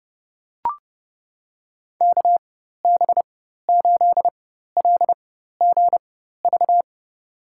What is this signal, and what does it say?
Morse code, heard as a keyed tone, about 30 wpm, spaced as for 14 wpm, 700 Hz. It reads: KB8LGV
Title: USA Call Signs - 6 Characters - 4x Character Spacing - 30wpm